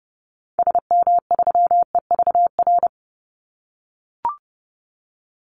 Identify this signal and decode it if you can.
Morse code, heard as a keyed tone, about 30 wpm, 700 Hz. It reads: SM3EVR
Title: Call Signs - 30wpm